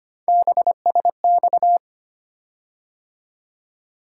Morse code, heard as a keyed tone, about 25 wpm, 700 Hz. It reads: BSX
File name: Three Letter-Number Combinations - 25wpm